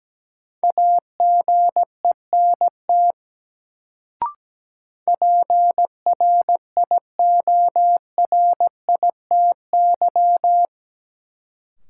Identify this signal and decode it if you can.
Morse code, heard as a keyed tone, about 17 wpm, 700 Hz. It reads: AGENT PRIORITY
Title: Sentences from Top 1600 Words - New Word Drill - 17wpm